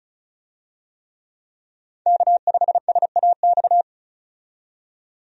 Morse code, heard as a keyed tone, about 36 wpm, 700 Hz. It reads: K5SAX
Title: USA Call Signs - 5 Characters - 35wpm